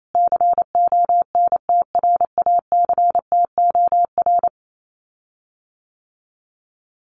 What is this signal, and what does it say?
Morse code, heard as a keyed tone, about 28 wpm, 700 Hz. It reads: CONTRACTOR
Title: Common QSO Phrases Encore 28wpm